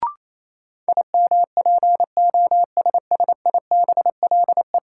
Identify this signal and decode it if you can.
Morse code, heard as a keyed tone, about 28 wpm, 700 Hz. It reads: IMPOSSIBLE
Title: Sentences from Top 1100 Words - New Word Drill - 28wpm